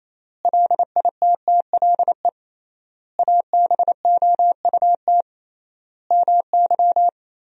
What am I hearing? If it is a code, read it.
Morse code, heard as a keyed tone, about 28 wpm, 700 Hz. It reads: LITTLE ABOUT MY
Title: 3x Word-Spacing - Sentences from Top 400 Words - 28wpm